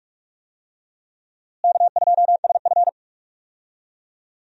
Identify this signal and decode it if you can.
Morse code, heard as a keyed tone, about 42 wpm, 700 Hz. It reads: K2SF
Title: USA Call Signs - 4 Characters - Repeated 3x - 45wpm